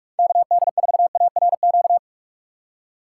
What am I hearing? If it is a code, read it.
Morse code, heard as a keyed tone, about 45 wpm, 700 Hz. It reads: KD4ARQ
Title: USA Call Signs - 6 Characters - 6x Character Spacing - 45wpm